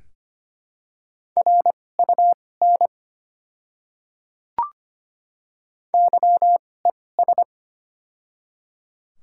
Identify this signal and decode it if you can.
Morse code, heard as a keyed tone, about 25 wpm, spaced as for 18 wpm, 700 Hz. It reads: RUN YES
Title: 2x Character Spacing - Top 500 Words - 25wpm